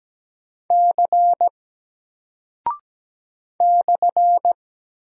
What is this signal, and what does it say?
Morse code, heard as a keyed tone, about 17 wpm, 700 Hz. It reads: C /
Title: Single Character (Letter:Number:Punctuation:Prosign) Alt 17wpm